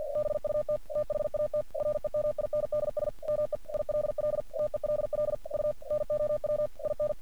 Morse code, heard as a keyed tone, about 47 wpm, 600 Hz. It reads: BUT THAT REMINDS ME ILL TELL U NOW IN
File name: archived / CRSnet